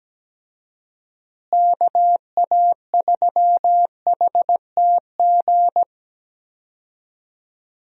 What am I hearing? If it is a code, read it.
Morse code, heard as a keyed tone, about 17 wpm, 700 Hz. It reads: KA3HTG
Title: USA Call Signs - 6 Characters - Repeated 3x - 17wpm